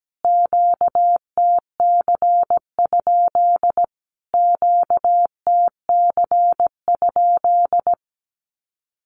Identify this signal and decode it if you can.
Morse code, heard as a keyed tone, about 17 wpm, 700 Hz. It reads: QTC? QTC?
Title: Q-Codes Repeated 3x 17wpm